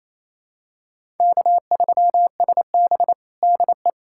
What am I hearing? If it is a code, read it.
Morse code, heard as a keyed tone, about 27 wpm, 700 Hz. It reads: K3SB DE
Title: USA Callsign DE Callsign - 4 Characters - 28wpm